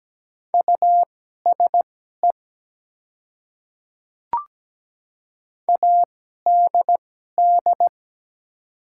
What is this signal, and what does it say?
Morse code, heard as a keyed tone, about 17 wpm, spaced as for 12 wpm, 700 Hz. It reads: USE ADD
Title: 2x Character Spacing - Top 200 Words - 17wpm